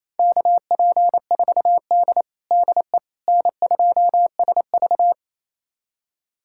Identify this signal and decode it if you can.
Morse code, heard as a keyed tone, about 28 wpm, 700 Hz. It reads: KP4D DE N2SV
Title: USA Callsign DE Callsign - 4 Characters - 28wpm